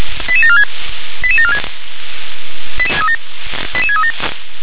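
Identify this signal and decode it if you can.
Five-tone selective call (ZVEI), heard as quick runs of five tones, tones of about 70 ms. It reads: 88746 88746 88637
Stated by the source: Five-tone ZVEI selective call received by SDR.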